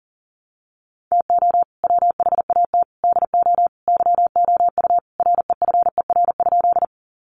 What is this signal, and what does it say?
Morse code, heard as a keyed tone, about 40 wpm, 700 Hz. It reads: TO WHAT DO YOU REFER?
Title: Sentences from Top 1000 Words 40wpm